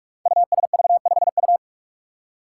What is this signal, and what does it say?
Morse code, heard as a keyed tone, about 44 wpm, 700 Hz. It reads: USV5U
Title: Five Challenging Letter-Number Combinations 45wpm